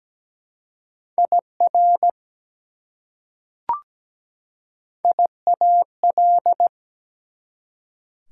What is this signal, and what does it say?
Morse code, heard as a keyed tone, about 17 wpm, 700 Hz. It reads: IR IAL
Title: Prefixes and Suffixes 17wpm